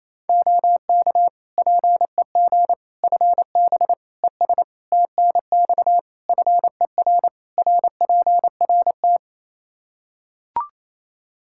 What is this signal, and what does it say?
Morse code, heard as a keyed tone, about 28 wpm, 700 Hz. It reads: OK PEG FB ES TNX FER RPRT
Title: Practice QSO - Exchange 2 - Info - 28wpm